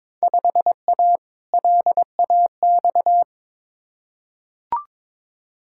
Morse code, heard as a keyed tone, about 22 wpm, 700 Hz. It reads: 5A LAX
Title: Field Day Exchanges - 22wpm